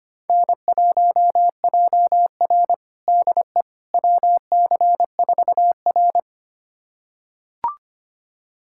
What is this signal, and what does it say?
Morse code, heard as a keyed tone, about 25 wpm, 700 Hz. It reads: N1JR DE WC4R
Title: USA Callsign DE Callsign - 4 Characters - 25wpm